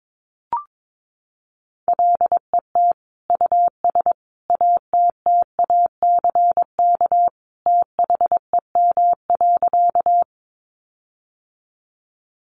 Morse code, heard as a keyed tone, about 22 wpm, 700 Hz. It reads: LET US ATTACK THEM.
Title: Sentences from Top 1500 Words 22wpm